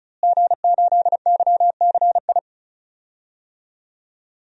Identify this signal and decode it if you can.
Morse code, heard as a keyed tone, about 36 wpm, 700 Hz. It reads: G8YCI
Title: DXCC-100 Call Signs - 5 Characters - Repeated 2x - 35wpm